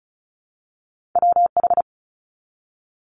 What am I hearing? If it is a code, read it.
Morse code, heard as a keyed tone, about 36 wpm, 700 Hz. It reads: WH
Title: Multi-letter Phonemes 35wpm